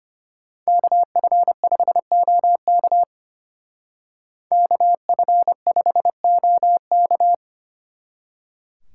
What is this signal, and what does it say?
Morse code, heard as a keyed tone, about 30 wpm, 700 Hz. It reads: KF5OK KF5OK
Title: Speed Racing - USA 2x2 Call Signs 25wpm